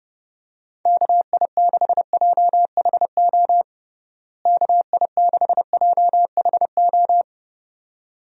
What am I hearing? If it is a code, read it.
Morse code, heard as a keyed tone, about 30 wpm, 700 Hz. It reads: KI6JHO KI6JHO
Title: USA Call Signs - 6 Characters - Repeated 2x - 30wpm